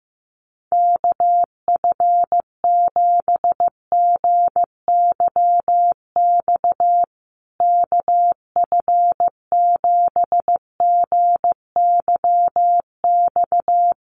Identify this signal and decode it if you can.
Morse code, heard as a keyed tone, about 15 wpm, 700 Hz. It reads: KF7GYX KF7GYX
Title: USA Call Signs Repeated 3x 15wpm